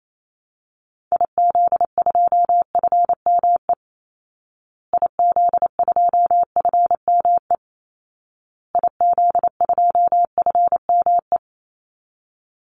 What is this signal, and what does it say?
Morse code, heard as a keyed tone, about 28 wpm, 700 Hz. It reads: IZ2FME IZ2FME IZ2FME
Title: Call Signs Repeated 2x - 28wpm